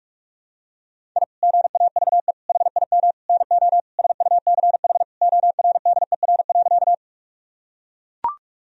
Morse code, heard as a keyed tone, about 41 wpm, 700 Hz. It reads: I GAVE HIM NO SUCH ORDER.
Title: Sentences from Top 1000 Words - Review Part 2 - 45wpm